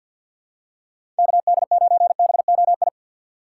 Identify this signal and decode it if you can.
Morse code, heard as a keyed tone, about 51 wpm, 700 Hz. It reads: KD9BOI
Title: USA Call Signs - 3x Character Spacing - 50wpm